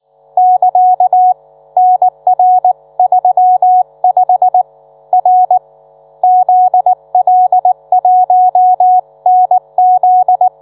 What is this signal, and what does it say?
Morse code, heard as a keyed tone, about 19 wpm, 730 Hz. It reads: <KA> NR35 R ZL1NZ